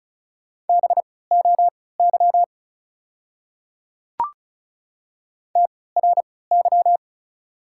Morse code, heard as a keyed tone, about 36 wpm, spaced as for 20 wpm, 690 Hz. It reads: BOY TRY
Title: 3x Character Spacing - Top 300 Words - 35wpm